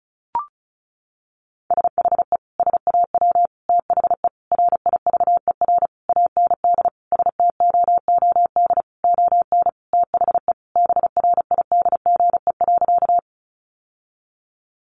Morse code, heard as a keyed tone, about 34 wpm, 700 Hz. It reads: SHE SAW THE RIVER AND STOOD ON THE BRIDGE.
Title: Sentences from Top 1200 Words 35wpm